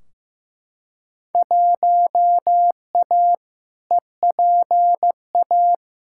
Morse code, heard as a keyed tone, about 15 wpm, 700 Hz. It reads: 1A EPA
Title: Field Day Exchanges - 15wpm